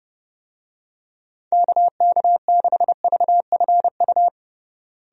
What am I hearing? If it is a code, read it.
Morse code, heard as a keyed tone, about 32 wpm, 700 Hz. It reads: KK6VFU
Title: USA Call Signs - 6 Characters - Repeated 3x - 30wpm